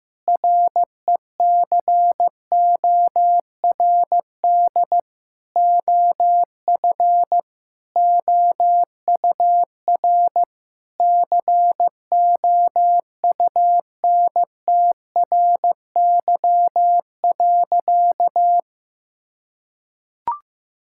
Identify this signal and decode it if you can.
Morse code, heard as a keyed tone, about 15 wpm, 700 Hz. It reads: RECORD OF OUR COUNTRY.
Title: Sentences from Top 1000 Words - Review Part 6 - 15wpm